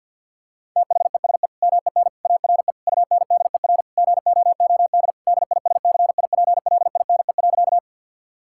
Morse code, heard as a keyed tone, about 50 wpm, 700 Hz. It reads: THESE MEN ARE UNDER GOOD DISCIPLINE.
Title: Sentences from Top 1000 Words - Review Part 6 - 50wpm